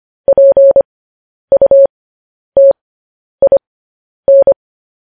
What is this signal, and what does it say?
Morse code, heard as a keyed tone, about 25 wpm, spaced as for 10 wpm, 550 Hz. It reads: PUTIN
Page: News Headlines in Morse Code at 10 WPM